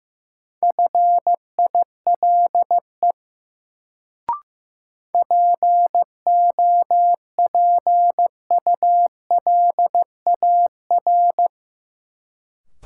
Morse code, heard as a keyed tone, about 16 wpm, 700 Hz. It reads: FILE POPULAR
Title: Sentences from Top 1200 Words - New Word Drill Alt - 15wpm